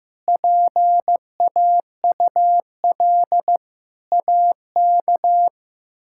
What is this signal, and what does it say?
Morse code, heard as a keyed tone, about 15 wpm, 700 Hz. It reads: PAUL AK
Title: K1USN SST QSO Practice - 15wpm